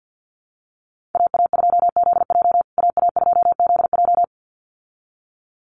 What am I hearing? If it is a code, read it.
Morse code, heard as a keyed tone, about 49 wpm, 700 Hz. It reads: AA2ZJ AA2ZJ
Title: USA Call Signs Repeated 2x 50wpm